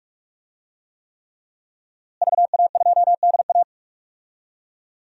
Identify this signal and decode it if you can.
Morse code, heard as a keyed tone, about 46 wpm, 700 Hz. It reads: VA2DA
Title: DXCC-100 Call Signs - 5 Characters - Repeated 3x - 45wpm